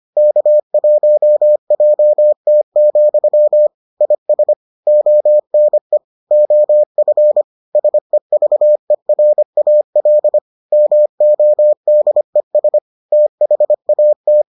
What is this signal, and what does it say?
Morse code, heard as a keyed tone, about 25 wpm, 600 Hz. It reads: K1JT, IS ONE OF SEVERAL MODES THAT